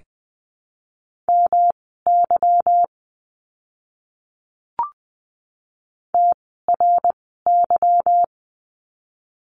Morse code, heard as a keyed tone, about 19 wpm, spaced as for 14 wpm, 700 Hz. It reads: MY TRY
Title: 206 Lesson 33f - Letter j - Words 20wpm